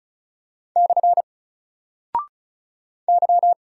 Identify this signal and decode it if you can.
Morse code, heard as a keyed tone, about 36 wpm, 700 Hz. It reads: / Y
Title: Single Character (Letter:Number:Punctuation:Prosign) 35wpm